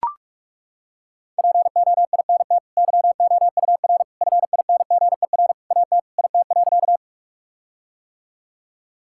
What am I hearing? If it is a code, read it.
Morse code, heard as a keyed tone, about 41 wpm, 700 Hz. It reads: POINT YOUR FINGER AT IT.
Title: Sentences from Top 1600 Words 45wpm